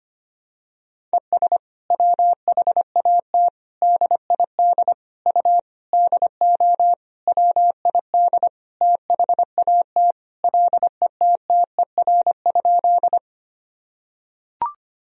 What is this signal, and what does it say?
Morse code, heard as a keyed tone, about 25 wpm, 700 Hz. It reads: ES WHAT DID U DO WID THAT LETTER?
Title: Sentences from Top 300 Words and Abbreviations 25wpm